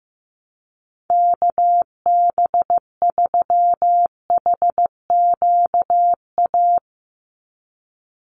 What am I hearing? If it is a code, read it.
Morse code, heard as a keyed tone, about 15 wpm, 700 Hz. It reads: KB3HQA
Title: USA Call Signs 15wpm